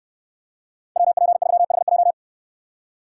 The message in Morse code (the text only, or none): WP3HJ